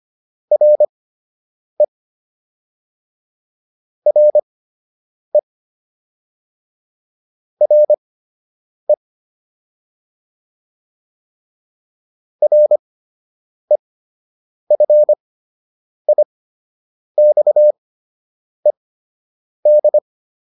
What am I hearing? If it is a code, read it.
Morse code, heard as a keyed tone, about 25 wpm, spaced as for 8 wpm, 600 Hz. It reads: RE RE RE REFIXED